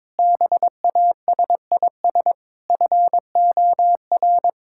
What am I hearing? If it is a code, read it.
Morse code, heard as a keyed tone, about 22 wpm, 700 Hz. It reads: BASIS FOR
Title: Sentences from Top 1600 Words 22wpm